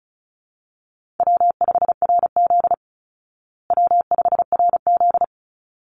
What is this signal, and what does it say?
Morse code, heard as a keyed tone, about 34 wpm, 700 Hz. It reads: W5RZ W5RZ
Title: Easy Call Signs Repeated 2x - 35wpm